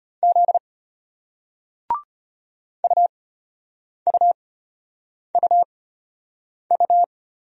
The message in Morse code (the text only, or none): Z UUUU